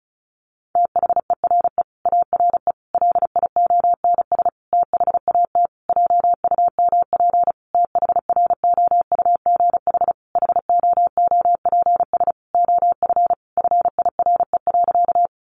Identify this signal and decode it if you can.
Morse code, heard as a keyed tone, about 34 wpm, 700 Hz. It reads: THERE ARE LIONS THAT JUMP THROUGH HOOPS OF FIRE.
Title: Easy Phrases 35wpm